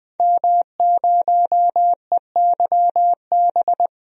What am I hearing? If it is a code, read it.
Morse code, heard as a keyed tone, about 20 wpm, 700 Hz. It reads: M0EYB
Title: DXCC-100 Call Signs - 5 Characters - 20wpm